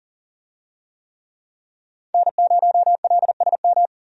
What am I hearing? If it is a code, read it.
Morse code, heard as a keyed tone, about 40 wpm, 700 Hz. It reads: N0LSM
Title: USA Call Signs - 5 Characters - 40wpm